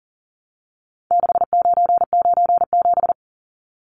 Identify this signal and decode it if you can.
Morse code, heard as a keyed tone, about 40 wpm, 700 Hz. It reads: 6997
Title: Numbers - 4-Digit 40wpm